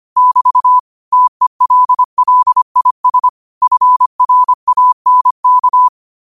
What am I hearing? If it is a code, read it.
Morse code, heard as a keyed tone, about 25 wpm, 1000 Hz. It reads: <BT> TELLIS FRANK
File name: qotdfiles / 25wpm